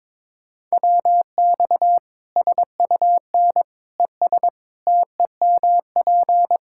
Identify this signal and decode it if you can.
Morse code, heard as a keyed tone, about 22 wpm, 700 Hz. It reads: WX SUN ES TEMP
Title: Practice QSO - Exchange 2 - Info - 22wpm